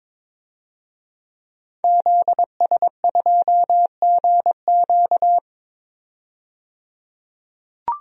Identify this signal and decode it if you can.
Morse code, heard as a keyed tone, about 22 wpm, 700 Hz. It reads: ZS2GQ
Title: DXCC-100 Call Signs - 5 Characters - 22wpm